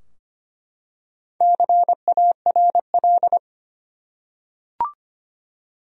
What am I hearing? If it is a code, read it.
Morse code, heard as a keyed tone, about 25 wpm, 700 Hz. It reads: CARL
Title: USA Common Names - Max 4 Letters - 25wpm